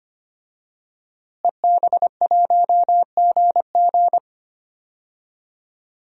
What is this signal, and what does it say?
Morse code, heard as a keyed tone, about 25 wpm, 700 Hz. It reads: EB1GG
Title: DXCC-100 Call Signs - 5 Characters - Repeated 2x - 25wpm